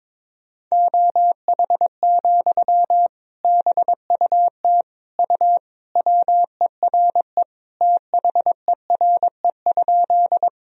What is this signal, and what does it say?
Morse code, heard as a keyed tone, about 22 wpm, 700 Hz. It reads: OH, BUT U WERE THERE?
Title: Sentences from Top 500 Words and Abbreviations 22wpm